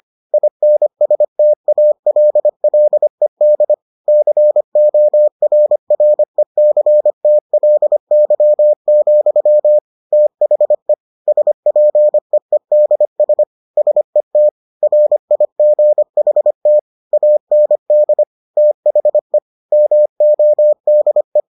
Morse code, heard as a keyed tone, about 25 wpm, 600 Hz. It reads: INSTALLED CORRECTLY, THE SPEEDS SET RIGHT AND THE MODE